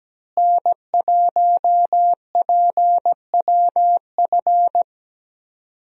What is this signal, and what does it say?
Morse code, heard as a keyed tone, about 17 wpm, 700 Hz. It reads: N1PWF
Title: USA Call Signs - 5 Characters - Repeated 3x - 17wpm